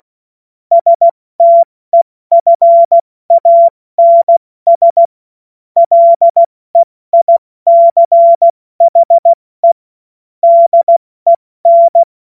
Morse code, heard as a keyed tone, about 16 wpm, spaced as for 14 wpm, 680 Hz. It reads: STEFANS LEICHE DEN
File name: Morsekurs / Witze / Witz 30 Januar 2024 70 F80